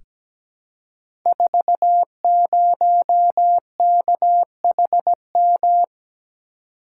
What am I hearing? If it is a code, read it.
Morse code, heard as a keyed tone, about 17 wpm, 700 Hz. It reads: 40KHM
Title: Code Groups - 5 Letter:Numbers - 17wpm